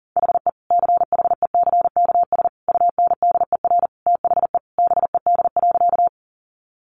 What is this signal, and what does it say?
Morse code, heard as a keyed tone, about 40 wpm, 700 Hz. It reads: HE CHECKS UNDER THE BED.
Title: Easy Phrases 40wpm